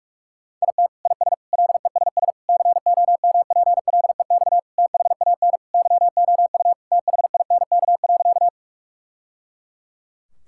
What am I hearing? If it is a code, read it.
Morse code, heard as a keyed tone, about 41 wpm, 700 Hz. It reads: IT IS LESS COMPLEX THAN YOU THINK.
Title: Sentences from Top 1000 Words - Review Part 6 - 45wpm